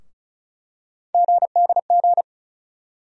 Morse code, heard as a keyed tone, about 34 wpm, 700 Hz. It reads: GDG
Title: USA Call Sign Postfixes - 3 Letters 35wpm